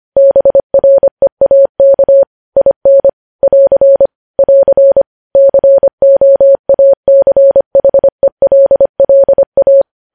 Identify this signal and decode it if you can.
Morse code, heard as a keyed tone, about 25 wpm, 550 Hz. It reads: BREAK IN <AR> <AR> COACHELLA
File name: podcast / News headlines-1x-25x25